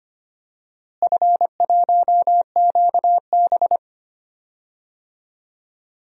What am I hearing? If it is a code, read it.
Morse code, heard as a keyed tone, about 25 wpm, 700 Hz. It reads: F1QB
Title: DXCC-100 Call Signs - 4 Characters - Repeated 2x - 25wpm